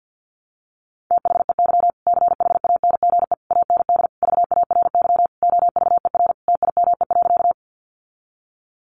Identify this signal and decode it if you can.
Morse code, heard as a keyed tone, about 50 wpm, 700 Hz. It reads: THEY CHANGE AND VARY OVER TIME.
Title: Sentences from Top 1200 Words 50wpm